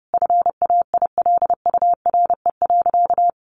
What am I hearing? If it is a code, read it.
Morse code, heard as a keyed tone, about 30 wpm, 700 Hz. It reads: FAILURE.
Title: Sentences from Top 1300 Words 30wpm